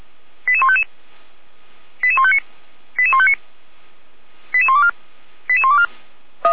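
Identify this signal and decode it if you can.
Five-tone selective call (ZVEI), heard as quick runs of five tones, tones of about 70 ms. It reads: 88155 88159 88159 88124 88124
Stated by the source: ZVEI five-tone selective call received on SDR.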